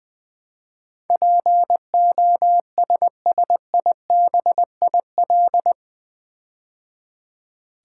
Morse code, heard as a keyed tone, about 20 wpm, 700 Hz. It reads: POSSIBIL
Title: Build A Word - 4 Letters At A Time - 11-Letter Words - 20wpm